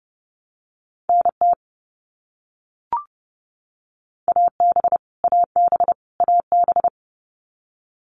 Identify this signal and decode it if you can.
Morse code, heard as a keyed tone, about 30 wpm, 700 Hz. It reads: NT AB AB AB
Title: Canadian Provinces and Territories - Repeated 3x - 30wpm